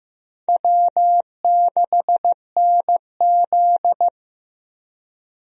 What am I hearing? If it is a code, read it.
Morse code, heard as a keyed tone, about 15 wpm, 700 Hz. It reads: W6NZ
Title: USA Call Signs - 4 Characters - Repeated 2x - 15wpm